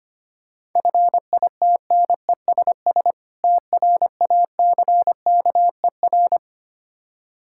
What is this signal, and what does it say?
Morse code, heard as a keyed tone, about 25 wpm, 700 Hz. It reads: FITNESS TRACKER
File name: practice / 25wpm / Difficult Words with Hints - 25wpm